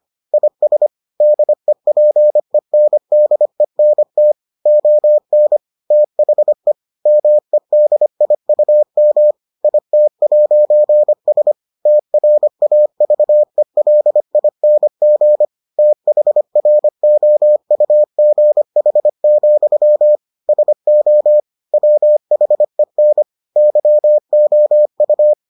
Morse code, heard as a keyed tone, about 25 wpm, 600 Hz. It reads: IS DEPENDENT ON THE MEDIUM IT'S TRAVELING THROUGH, SO WHEN YOU